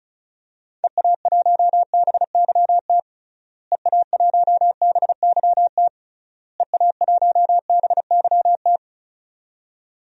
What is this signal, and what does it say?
Morse code, heard as a keyed tone, about 34 wpm, 700 Hz. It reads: EA1BYT EA1BYT EA1BYT
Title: DXCC-100 Call Signs - 6 Characters - Repeated 3x - 35wpm